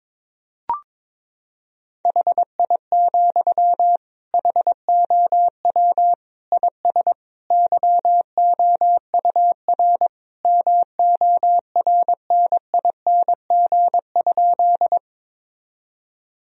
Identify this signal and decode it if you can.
Morse code, heard as a keyed tone, about 22 wpm, 700 Hz. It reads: HI, HOW IS YOUR MORNING?
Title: Sentences from Top 1000 Words - Review Part 3 - 22wpm